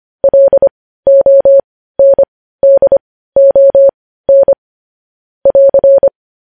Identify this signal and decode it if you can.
Morse code, heard as a keyed tone, about 25 wpm, spaced as for 15 wpm, 550 Hz. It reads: LONDON <AR>
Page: News Headlines in Morse Code at 15 WPM